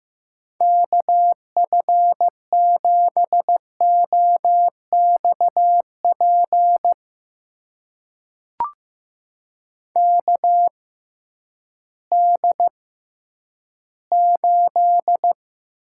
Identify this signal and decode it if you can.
Morse code, heard as a keyed tone, about 15 wpm, spaced as for 10 wpm, 700 Hz. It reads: KF7OXP KD8
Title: USA Call Signs - 6 Characters - 6x Character Spacing - 15wpm